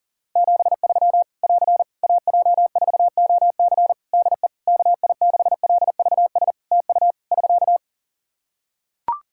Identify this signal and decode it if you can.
Morse code, heard as a keyed tone, about 40 wpm, 700 Hz. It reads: 73 <AR> AJ4OC DE KI6LVS TU <SK>
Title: Practice QSO - Exchange 4 - Ending - 40wpm